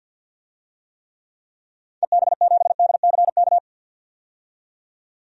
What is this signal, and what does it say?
Morse code, heard as a keyed tone, about 52 wpm, 700 Hz. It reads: EB7DCK
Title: DXCC-100 Call Signs - 6 Characters - 50wpm